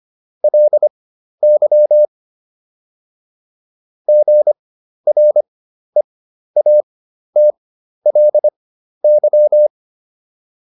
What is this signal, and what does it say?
Morse code, heard as a keyed tone, about 25 wpm, spaced as for 12 wpm, 600 Hz. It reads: LY GREATLY